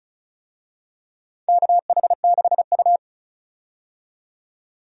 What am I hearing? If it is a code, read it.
Morse code, heard as a keyed tone, about 36 wpm, 690 Hz. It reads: KH6U
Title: DXCC-100 Call Signs - 4 Characters - 35wpm